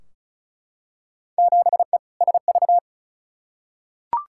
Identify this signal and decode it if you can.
Morse code, heard as a keyed tone, about 36 wpm, 700 Hz. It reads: 7E SV
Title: Field Day Exchanges Encore - 35wpm